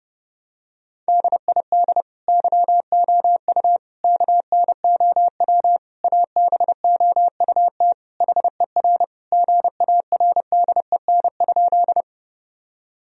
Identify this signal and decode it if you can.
Morse code, heard as a keyed tone, about 30 wpm, 700 Hz. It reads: DID YOU KNOW ABOUT HER GARDEN?
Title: Sentences from Top 1000 Words - Review Part 4 - 30wpm